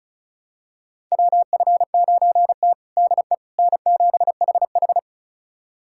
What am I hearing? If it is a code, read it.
Morse code, heard as a keyed tone, about 34 wpm, 700 Hz. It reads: WF9T DE N7HH
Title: USA Callsign DE Callsign - 4 Characters - 35wpm